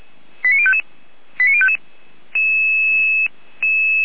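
Five-tone selective call (ZVEI), heard as quick runs of five tones, tones of about 70 ms. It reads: 89055 89055
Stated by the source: ZVEI five-tone selective call received by SDR.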